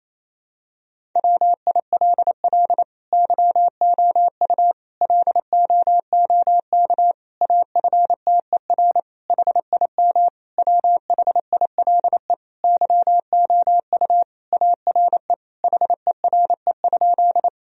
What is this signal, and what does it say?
Morse code, heard as a keyed tone, about 28 wpm, 700 Hz. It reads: WILL YOU LOOK AFTER HIM WHILE YOU ARE HERE?
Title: Sentences from Top 1000 Words - Review Part 1 - 28wpm